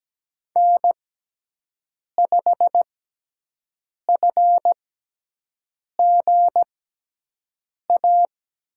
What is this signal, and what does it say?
Morse code, heard as a keyed tone, about 17 wpm, spaced as for 6 wpm, 700 Hz. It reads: N5FGA
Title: USA Call Signs - 5 Characters - 6x Character Spacing - 17wpm